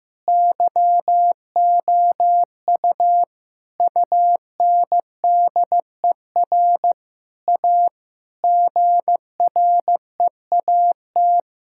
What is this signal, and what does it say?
Morse code, heard as a keyed tone, about 15 wpm, 700 Hz. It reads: YOU UNDER A GREAT
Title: Sentences from Top 1000 Words - Review Part 6 - 15wpm